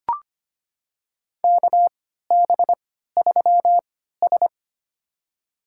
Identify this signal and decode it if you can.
Morse code, heard as a keyed tone, about 25 wpm, spaced as for 14 wpm, 700 Hz. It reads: KB3S
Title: USA Call Signs - 4 Characters - 3x Character Spacing - 25wpm